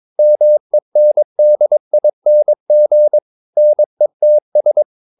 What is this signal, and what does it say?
Morse code, heard as a keyed tone, about 22 wpm, 600 Hz. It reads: MENDING NETS
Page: Romanen Never af Ken Follet på engelsk.